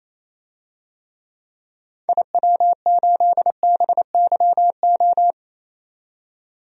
Morse code, heard as a keyed tone, about 28 wpm, 700 Hz. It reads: IW8BYO
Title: DXCC-100 Call Signs - 6 Characters - 28wpm